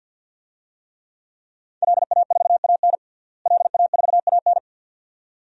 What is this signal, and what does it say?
Morse code, heard as a keyed tone, about 52 wpm, 700 Hz. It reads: LA4AN LA4AN
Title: DXCC-100 Call Signs - 5 Characters - Repeated 2x - 50wpm